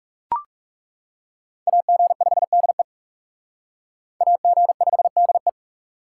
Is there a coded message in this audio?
Yes: Morse code from 1.7 to 5.5 s, heard as a keyed tone, about 38 wpm, 700 Hz. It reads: AG5DE AG5DE